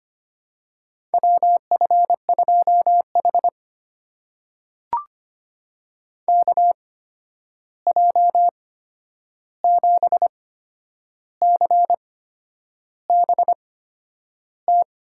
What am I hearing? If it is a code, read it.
Morse code, heard as a keyed tone, about 25 wpm, spaced as for 9 wpm, 700 Hz. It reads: WF2H KJ7CBT
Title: USA Call Signs - 8x Character Spacing - 25wpm